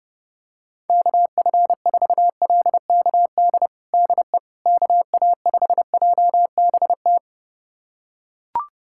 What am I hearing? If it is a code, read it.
Morse code, heard as a keyed tone, about 30 wpm, 700 Hz. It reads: KF4LKD DE KA5JBT
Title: USA Callsign DE Callsign - 6 Characters - 30wpm